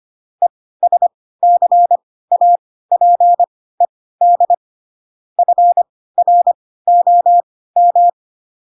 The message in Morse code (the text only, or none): ESCAPED FROM